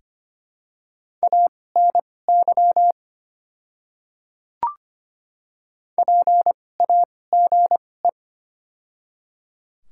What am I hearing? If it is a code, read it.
Morse code, heard as a keyed tone, about 25 wpm, spaced as for 18 wpm, 700 Hz. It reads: ANY PAGE
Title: 2x Character Spacing - Top 300 Words - 25wpm